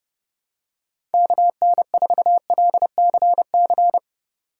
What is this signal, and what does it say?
Morse code, heard as a keyed tone, about 30 wpm, 700 Hz. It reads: KN4LCC
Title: USA Call Signs - 3x Character Spacing - 30wpm